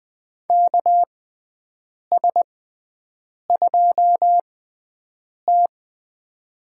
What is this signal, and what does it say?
Morse code, heard as a keyed tone, about 20 wpm, spaced as for 7 wpm, 700 Hz. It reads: KS2T
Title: USA Call Signs - 4 Characters - 6x Character Spacing - 20wpm